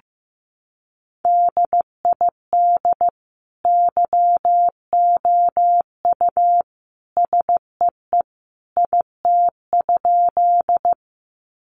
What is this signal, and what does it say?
Morse code, heard as a keyed tone, about 15 wpm, 700 Hz. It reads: DID YOU SEE IT?
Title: Sentences from Top 100 Words 15wpm